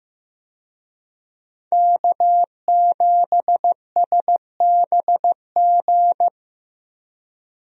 Morse code, heard as a keyed tone, about 15 wpm, 700 Hz. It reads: K7SBG
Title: USA Call Signs - 5 Characters - Repeated 2x - 15wpm